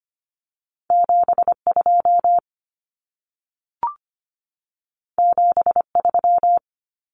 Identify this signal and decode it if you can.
Morse code, heard as a keyed tone, about 25 wpm, 700 Hz. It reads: 72 73
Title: Numbers - 2-Digit 25wpm